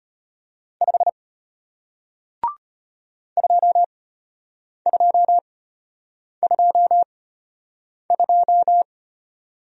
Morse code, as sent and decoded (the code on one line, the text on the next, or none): ..... / ..--- ..--- ..--- ..---
5 2222